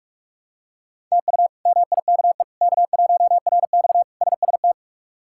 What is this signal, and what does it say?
Morse code, heard as a keyed tone, about 46 wpm, 700 Hz. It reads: TU MIKE K1RX SST
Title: K1USN SST QSO Practice - 45wpm